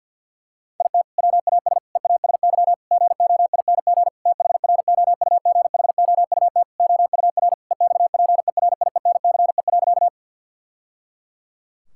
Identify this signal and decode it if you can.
Morse code, heard as a keyed tone, about 50 wpm, 700 Hz. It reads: IT WAS EASY GOING THROUGHOUT OUR EXPERIENCE.
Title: Sentences from Top 1000 Words - Review Part 5 - 50wpm